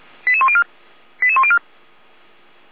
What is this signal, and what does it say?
Five-tone selective call (ZVEI), heard as quick runs of five tones, tones of about 70 ms. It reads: 88104 88104